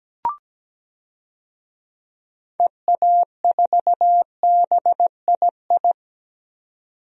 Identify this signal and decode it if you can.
Morse code, heard as a keyed tone, about 17 wpm, 700 Hz. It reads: EA4BII
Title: DXCC-100 Call Signs - 6 Characters - 17wpm